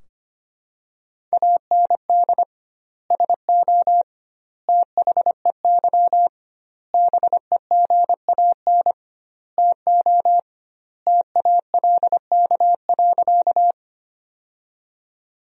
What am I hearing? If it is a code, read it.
Morse code, heard as a keyed tone, about 25 wpm, 700 Hz. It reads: AND SO THEY BEGAN TO TALK.
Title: 2x Word-Spacing - Sentences from Top 400 Words - 25wpm